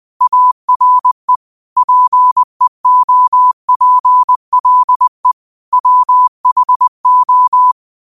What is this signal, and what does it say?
Morse code, heard as a keyed tone, about 20 wpm, 1000 Hz. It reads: ARE PEOPLE WHO